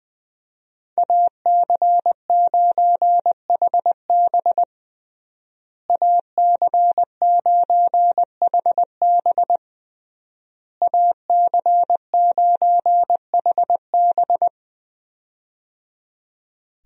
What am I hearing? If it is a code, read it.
Morse code, heard as a keyed tone, about 20 wpm, 700 Hz. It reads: AC9HB AC9HB AC9HB
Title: USA Call Signs - 5 Characters - Repeated 3x - 20wpm